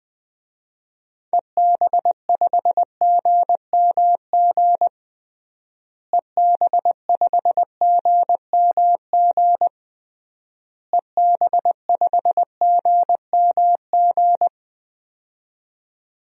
Morse code, heard as a keyed tone, about 20 wpm, 700 Hz. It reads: EB5GMG EB5GMG EB5GMG
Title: DXCC-100 Call Signs - 6 Characters - Repeated 3x - 20wpm